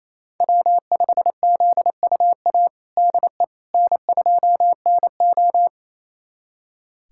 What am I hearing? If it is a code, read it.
Morse code, heard as a keyed tone, about 28 wpm, 700 Hz. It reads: W5ZUA DE N2NO
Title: USA Callsign DE Callsign 28wpm